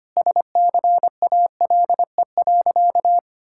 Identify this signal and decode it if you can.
Morse code, heard as a keyed tone, about 25 wpm, 700 Hz. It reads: SCALE.
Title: Sentences from Top 1000 Words - Review Part 5 - 25wpm